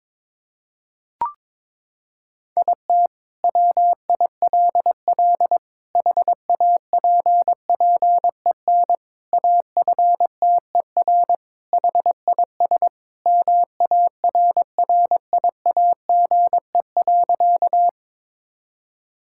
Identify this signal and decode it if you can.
Morse code, heard as a keyed tone, about 22 wpm, 700 Hz. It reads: IT WILL HAPPEN AFTER HIS MARRIAGE.
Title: Sentences from Top 1000 Words - Review Part 5 - 22wpm